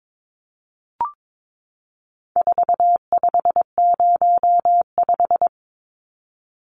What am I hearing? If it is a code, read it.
Morse code, heard as a keyed tone, about 22 wpm, 700 Hz. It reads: 4505
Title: Numbers - 4-Digit 22wpm